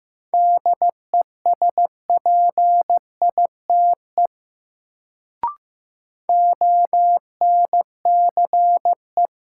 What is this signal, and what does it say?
Morse code, heard as a keyed tone, about 15 wpm, 700 Hz. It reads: DESPITE ONCE
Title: Sets of 4 Words - New Word Drill - 15wpm